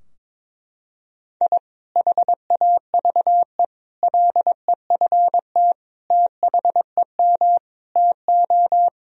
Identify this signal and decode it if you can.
Morse code, heard as a keyed tone, about 22 wpm, 700 Hz. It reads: I HAVE LEFT THEM TO
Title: Sentences from Top 1000 Words - Review Part 2 - 22wpm